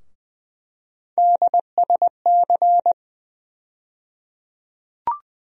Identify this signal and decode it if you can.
Morse code, heard as a keyed tone, about 20 wpm, 700 Hz. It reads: DSC
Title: USA Call Sign Postfixes - 3 Letters 20wpm